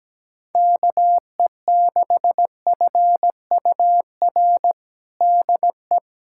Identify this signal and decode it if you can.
Morse code, heard as a keyed tone, about 17 wpm, 700 Hz. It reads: KE6FUR DE
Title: USA Callsign DE Callsign 17wpm